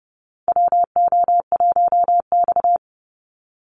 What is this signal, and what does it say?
Morse code, heard as a keyed tone, about 30 wpm, 700 Hz. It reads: WO1X
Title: Call Signs Repeated 2x - 30wpm